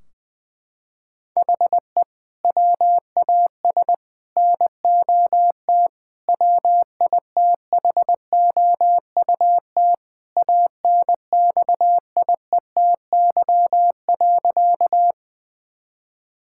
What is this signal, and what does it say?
Morse code, heard as a keyed tone, about 20 wpm, 700 Hz. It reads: HE WAS NOT WITHOUT ANXIETY.
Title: Sentences from Top 1600 Words 20wpm